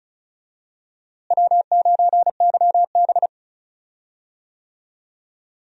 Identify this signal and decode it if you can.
Morse code, heard as a keyed tone, about 34 wpm, 700 Hz. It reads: W9YB
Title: USA Call Signs - 4 Characters - Repeated 2x - 35wpm